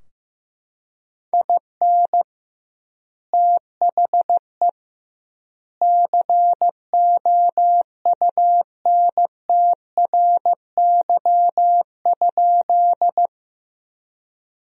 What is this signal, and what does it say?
Morse code, heard as a keyed tone, about 16 wpm, 700 Hz. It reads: IN THE COUNTRY?
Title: 2x Word-Spacing - Sentences from Top 300 Words - 15wpm